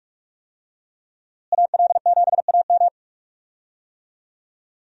Morse code, heard as a keyed tone, about 46 wpm, 700 Hz. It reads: AL7AM
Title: USA Call Signs - 5 Characters - Repeated 2x - 45wpm